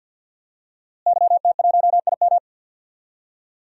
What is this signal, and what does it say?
Morse code, heard as a keyed tone, about 48 wpm, 700 Hz. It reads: YT1IM
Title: DXCC-100 Call Signs - 5 Characters - Repeated 3x - 50wpm